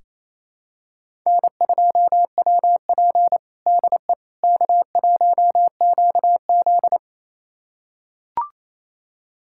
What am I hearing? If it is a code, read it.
Morse code, heard as a keyed tone, about 27 wpm, 700 Hz. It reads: N2WP DE K1QZ
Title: USA Callsign DE Callsign - 4 Characters - 28wpm